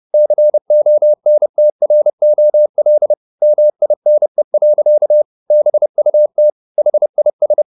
Morse code, heard as a keyed tone, about 30 wpm, 600 Hz. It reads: CONTROL MINE. BUT HIS